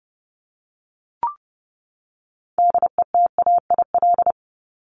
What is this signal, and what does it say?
Morse code, heard as a keyed tone, about 30 wpm, 700 Hz. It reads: DETAIL
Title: Top 1251-1500 Words 30wpm